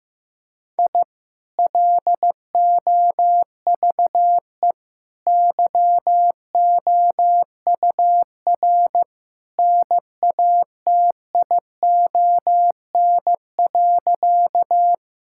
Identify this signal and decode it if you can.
Morse code, heard as a keyed tone, about 16 wpm, 700 Hz. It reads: I LOVE YOUR NATION.
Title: Sentences from Top 1000 Words - Review Part 6 - 15wpm